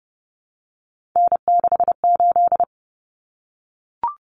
Morse code, heard as a keyed tone, about 30 wpm, 700 Hz. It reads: N68
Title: Common Cut Numbers - 3-Digit 30wpm